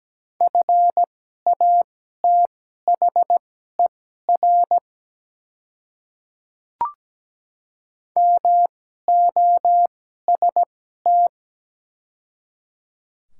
Morse code, heard as a keyed tone, about 17 wpm, spaced as for 12 wpm, 700 Hz. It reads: FATHER MOST
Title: 2x Character Spacing - Top 300 Words - 17wpm